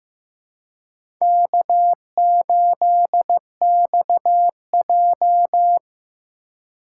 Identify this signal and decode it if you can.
Morse code, heard as a keyed tone, about 15 wpm, 700 Hz. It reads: K8XJ
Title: K1USN SST QSO Practice - 15wpm